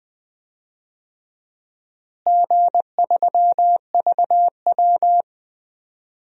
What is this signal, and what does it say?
Morse code, heard as a keyed tone, about 21 wpm, 700 Hz. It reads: G3VW
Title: DXCC-100 Call Signs - 4 Characters - Repeated 3x - 20wpm